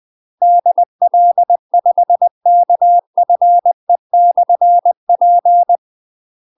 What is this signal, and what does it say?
Morse code, heard as a keyed tone, about 20 wpm, 700 Hz. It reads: DL5KFE/P
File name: UK MASTER QSOs TRAINING / 20WPM / UK MASTER QSOs 0053